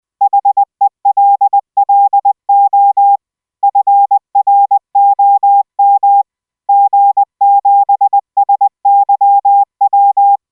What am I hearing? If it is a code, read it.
Morse code, heard as a keyed tone, about 20 wpm, 800 Hz. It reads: HELLO FROM G7SYW